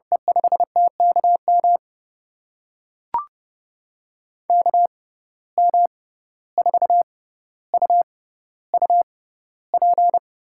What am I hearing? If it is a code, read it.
Morse code, heard as a keyed tone, about 31 wpm, spaced as for 15 wpm, 700 Hz. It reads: E5TKM KM4UUP
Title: USA Call Signs - 6 Characters - 6x Character Spacing - 30wpm